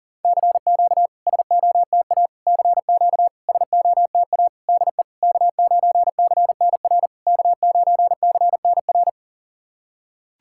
Morse code, heard as a keyed tone, about 40 wpm, 690 Hz. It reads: CQ SOTA CQ SOTA DE K9CNR K9CNR
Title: SOTA QSO Practice 40wpm